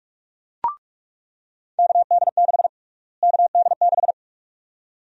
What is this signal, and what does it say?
Morse code, heard as a keyed tone, about 46 wpm, 700 Hz. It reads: KD6 KD6
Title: USA Call Sign Prefixes - Repeated 2x - 45wpm